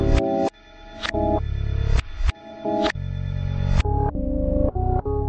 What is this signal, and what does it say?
There is no code.